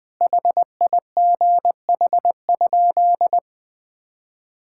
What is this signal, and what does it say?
Morse code, heard as a keyed tone, about 21 wpm, 700 Hz. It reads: HIGH?
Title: Sentences from Top 1000 Words - Review Part 5 - 20wpm